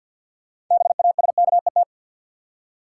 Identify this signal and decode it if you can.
Morse code, heard as a keyed tone, about 47 wpm, 700 Hz. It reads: BASKET
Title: Six Letter Words - 50wpm